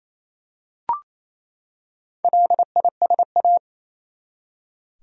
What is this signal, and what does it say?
Morse code, heard as a keyed tone, about 28 wpm, 700 Hz. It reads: LISA
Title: USA Common Names - Max 4 Letters - 28wpm